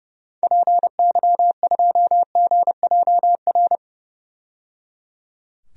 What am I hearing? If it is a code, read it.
Morse code, heard as a keyed tone, about 30 wpm, 700 Hz. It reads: PY2GJR
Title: DXCC-100 Call Signs - 6 Characters - 30wpm